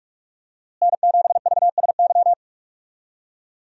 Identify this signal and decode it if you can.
Morse code, heard as a keyed tone, about 46 wpm, 700 Hz. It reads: N7VSY
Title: USA Call Signs - 5 Characters - Repeated 3x - 45wpm